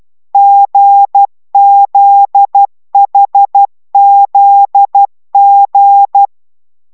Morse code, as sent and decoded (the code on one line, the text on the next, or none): --. --.. .... --.. --.
GZHZG